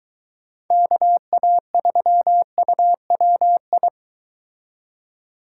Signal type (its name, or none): Morse code